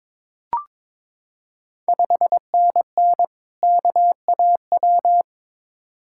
Morse code, heard as a keyed tone, about 22 wpm, 690 Hz. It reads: 5NN KAW
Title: Ontario QSO Party Exchange 22wpm